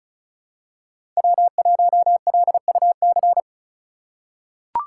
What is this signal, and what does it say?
Morse code, heard as a keyed tone, about 34 wpm, 700 Hz. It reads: W1LUC
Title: USA Call Signs - 4x Character Spacing - 35wpm